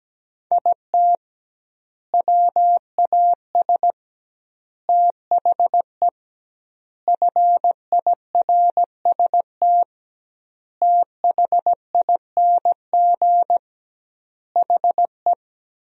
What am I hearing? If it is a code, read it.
Morse code, heard as a keyed tone, about 17 wpm, 700 Hz. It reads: IT WAS THE FIRST THING HE
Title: 2x Word-Spacing - Sentences from Top 300 Words - 17wpm